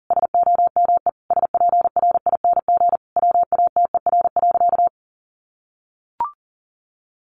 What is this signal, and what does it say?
Morse code, heard as a keyed tone, about 40 wpm, 700 Hz. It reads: SOME SPRING WATER.
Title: Sentences from Top 1200 Words 40wpm